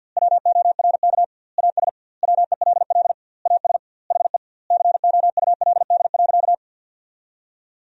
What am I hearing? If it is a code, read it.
Morse code, heard as a keyed tone, about 50 wpm, 700 Hz. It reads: WORK AS WELL AS HE COULD.
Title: 2x Word-Spacing - Sentences from Top 500 Words - 50wpm